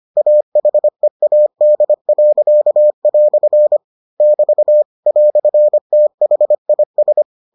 Morse code, heard as a keyed tone, about 25 wpm, 600 Hz. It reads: AHEAD." <BT> "THIS